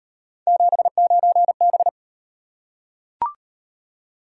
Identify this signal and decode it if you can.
Morse code, heard as a keyed tone, about 37 wpm, 700 Hz. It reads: 79B